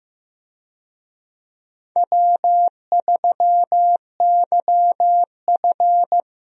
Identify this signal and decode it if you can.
Morse code, heard as a keyed tone, about 15 wpm, 700 Hz. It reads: W3YF
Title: USA Call Signs - 4 Characters - 15wpm